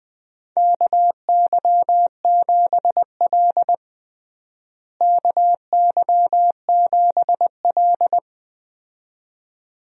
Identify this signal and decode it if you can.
Morse code, heard as a keyed tone, about 20 wpm, 700 Hz. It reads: KY7L KY7L
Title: USA Call Signs - 4 Characters - Repeated 2x - 20wpm